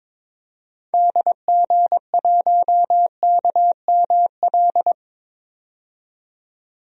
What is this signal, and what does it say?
Morse code, heard as a keyed tone, about 22 wpm, 700 Hz. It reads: DG1KML
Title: DXCC-100 Call Signs - 6 Characters - 22wpm